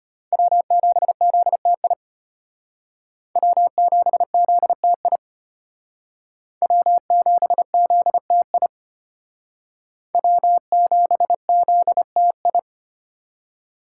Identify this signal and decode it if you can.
Morse code, heard as a keyed tone, about 32 wpm, 700 Hz. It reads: W7ZTI W7ZTI W7ZTI W7ZTI